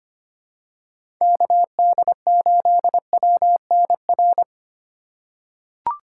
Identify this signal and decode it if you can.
Morse code, heard as a keyed tone, about 25 wpm, 700 Hz. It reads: KD8WNR
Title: USA Call Signs - 6 Characters - 2x Character Spacing - 25wpm